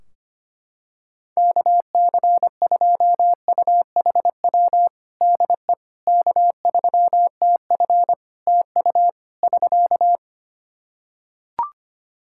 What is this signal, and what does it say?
Morse code, heard as a keyed tone, about 25 wpm, 700 Hz. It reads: KC2UHW DE K3TF TU <SK>
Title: Practice QSO - Exchange 4 - Ending - 25wpm